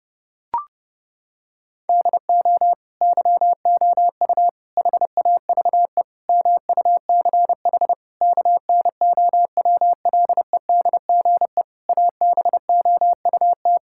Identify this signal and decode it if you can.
Morse code, heard as a keyed tone, about 32 wpm, 700 Hz. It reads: DO YOU HAVE MUCH KNOWLEDGE ABOUT
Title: Sentences from Top 1000 Words - Review Part 6 - 30wpm